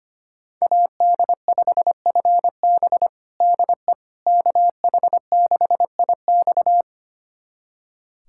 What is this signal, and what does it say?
Morse code, heard as a keyed tone, about 25 wpm, 700 Hz. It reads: AD5FB DE KH6IX
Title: USA Callsign DE Callsign - 5 Characters - 25wpm